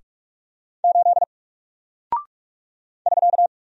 Morse code, heard as a keyed tone, about 44 wpm, 700 Hz. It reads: 8 <SK>
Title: Single Character (Letter:Number:Punctuation:Prosign) Alt 45wpm